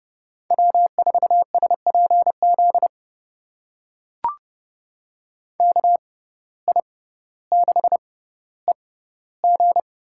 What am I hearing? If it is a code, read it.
Morse code, heard as a keyed tone, about 30 wpm, spaced as for 15 wpm, 700 Hz. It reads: W4SPZ KI6EG
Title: USA Call Signs - 5 Characters - 6x Character Spacing - 30wpm